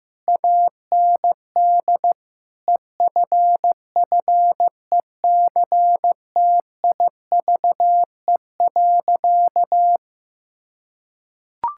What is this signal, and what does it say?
Morse code, heard as a keyed tone, about 15 wpm, 700 Hz. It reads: AND EFFECTIVE.
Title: Sentences from Top 1600 Words 15wpm